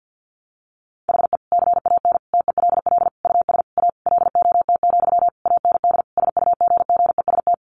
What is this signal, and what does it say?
Morse code, heard as a keyed tone, about 50 wpm, 700 Hz. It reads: HE CAN TELL US A LOT, AND SUGGEST
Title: Sentences from Top 1300 Words 50wpm